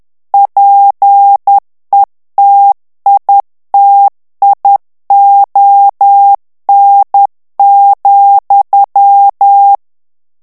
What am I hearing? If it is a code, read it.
Morse code, heard as a keyed tone, about 10 wpm, 800 Hz. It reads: PETITION,
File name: ANGLAIS texte clair 3 vitesse 10 mots minute